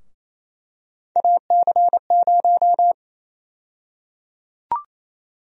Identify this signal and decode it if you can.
Morse code, heard as a keyed tone, about 28 wpm, 700 Hz. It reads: AC0
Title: USA Call Sign Prefixes - Repeated 2x - 28wpm